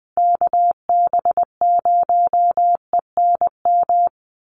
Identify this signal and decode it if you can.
Morse code, heard as a keyed tone, about 20 wpm, 700 Hz. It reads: KB0ENM
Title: USA Call Signs Repeated 2x 20wpm